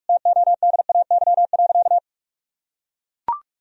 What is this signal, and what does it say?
Morse code, heard as a keyed tone, about 44 wpm, 700 Hz. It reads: TODAY.
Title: Sentences from Top 1000 Words - Review Part 4 - 45wpm